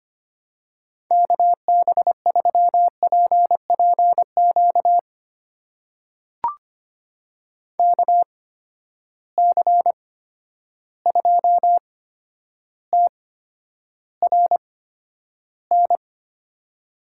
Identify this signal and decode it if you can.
Morse code, heard as a keyed tone, about 25 wpm, spaced as for 11 wpm, 700 Hz. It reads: KB3PPQ KC2TRN
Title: USA Call Signs - 6 Characters - 8x Character Spacing - 25wpm